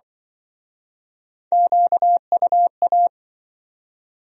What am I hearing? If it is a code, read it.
Morse code, heard as a keyed tone, about 26 wpm, 700 Hz. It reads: QUA